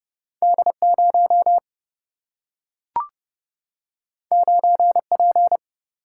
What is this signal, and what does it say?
Morse code, heard as a keyed tone, about 31 wpm, 700 Hz. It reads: D 0 9 P
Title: Single-Letter Single-Number 30wpm